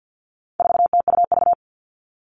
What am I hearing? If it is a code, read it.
Morse code, heard as a keyed tone, about 50 wpm, 700 Hz. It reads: 4TUV
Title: All Cut Numbers - 4-Digit 50wpm